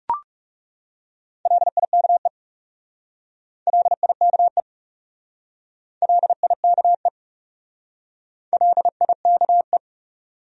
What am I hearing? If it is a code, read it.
Morse code, heard as a keyed tone, about 38 wpm, 700 Hz. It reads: LIKE LIKE LIKE LIKE